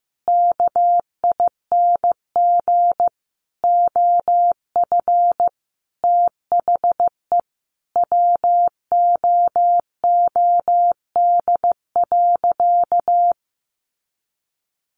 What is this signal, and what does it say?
Morse code, heard as a keyed tone, about 15 wpm, 700 Hz. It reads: KING OF THE WOOD.
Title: Sentences from Top 1200 Words 15wpm